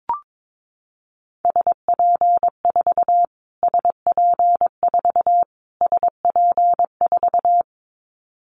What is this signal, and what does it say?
Morse code, heard as a keyed tone, about 22 wpm, 700 Hz. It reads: SP4 SP4 SP4
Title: Callsign Prefixes for Top 25 DXCC - Repeated 3x - 22wpm